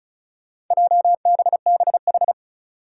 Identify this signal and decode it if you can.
Morse code, heard as a keyed tone, about 36 wpm, 700 Hz. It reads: JBBH
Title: Four Challenging Letter-Number Combinations 35wpm